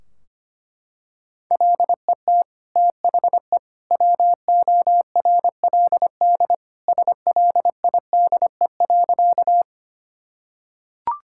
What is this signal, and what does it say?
Morse code, heard as a keyed tone, about 25 wpm, 700 Hz. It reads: LET THE WORLD SLIDE.
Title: Sentences from Top 1600 Words 25wpm